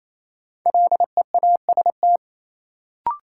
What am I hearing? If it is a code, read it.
Morse code, heard as a keyed tone, about 28 wpm, 700 Hz. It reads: LEAST
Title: Sentences from Top 700 Words - New Word Drill - 28wpm